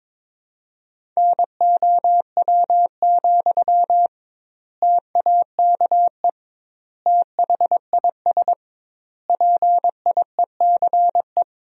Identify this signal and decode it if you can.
Morse code, heard as a keyed tone, about 22 wpm, 700 Hz. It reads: NOW, TAKE THIS PIECE
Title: 2x Word-Spacing - Sentences from Top 400 Words - 22wpm